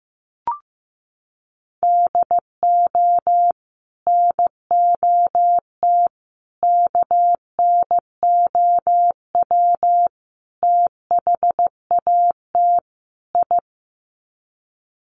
Morse code, heard as a keyed tone, about 15 wpm, 700 Hz. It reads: DO NOT KNOW THAT I
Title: Sets of 5 Words Encore2 15wpm